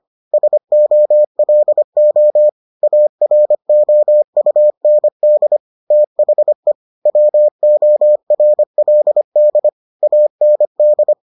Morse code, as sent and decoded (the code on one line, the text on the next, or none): ... --- .-.. --- / .- .-. --- ..- -. -.. / - .... / .-- --- .-. .-.. -.. / .- -. -..
SOLO AROUND THE WORLD AND